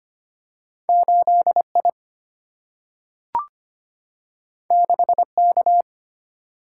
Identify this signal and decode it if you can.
Morse code, heard as a keyed tone, about 25 wpm, 700 Hz. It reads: 8I 6K
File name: Two Letter-Number Combinations - 25wpm